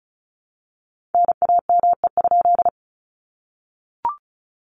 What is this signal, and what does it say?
Morse code, heard as a keyed tone, about 34 wpm, 700 Hz. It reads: NAME?
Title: Common QSO Phrases 35wpm